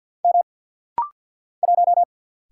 Morse code, heard as a keyed tone, about 50 wpm, 700 Hz. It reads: M 1
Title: Single Letter-Number - Rapid-Fire 50wpm